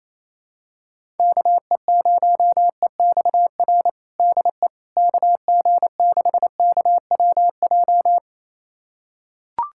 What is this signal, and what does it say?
Morse code, heard as a keyed tone, about 27 wpm, 700 Hz. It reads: KE0EXR DE KG6KWJ
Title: USA Callsign DE Callsign - 6 Characters - 28wpm